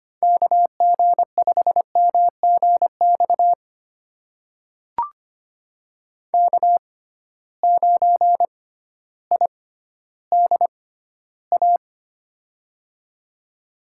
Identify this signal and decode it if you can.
Morse code, heard as a keyed tone, about 25 wpm, spaced as for 13 wpm, 700 Hz. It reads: KG5MGX K9IDA
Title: USA Call Signs - 6x Character Spacing - 25wpm